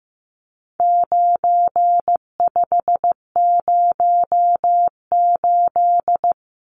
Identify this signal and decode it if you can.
Morse code, heard as a keyed tone, about 14 wpm, 700 Hz. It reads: 9508
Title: Numbers - 4-Digit 15wpm